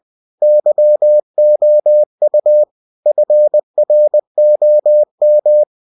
Morse code, heard as a keyed tone, about 20 wpm, 600 Hz. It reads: YOU FROM
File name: cw-books / winnie-the-pooh / winnie-20wpm-page-0114